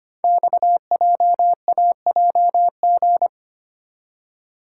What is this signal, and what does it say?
Morse code, heard as a keyed tone, about 25 wpm, 700 Hz. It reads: XJAJG
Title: Code Groups - 5 Letters - 25wpm